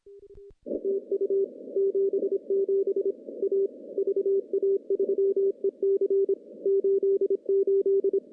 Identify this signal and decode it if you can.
Morse code, heard as a keyed tone, about 25 wpm, 410 Hz. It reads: TU 77 A VA3EC 88
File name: contests / CW-SS-k5tr- -0200